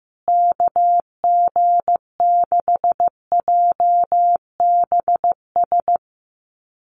Morse code, heard as a keyed tone, about 15 wpm, 700 Hz. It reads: KG6JBS
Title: USA Call Signs Repeated 3x 15wpm